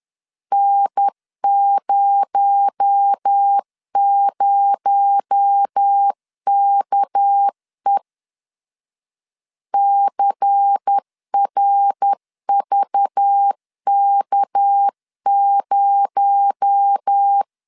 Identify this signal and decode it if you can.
Morse code, heard as a keyed tone, about 10 wpm, 800 Hz. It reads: N00KE CRVK0
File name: LEZIONI / 6 PKX0 / 6 PKX0 10wpm 2707